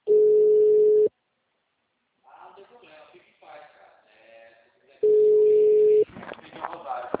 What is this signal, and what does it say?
Ringback tone (Europe, 425 Hz), with speech.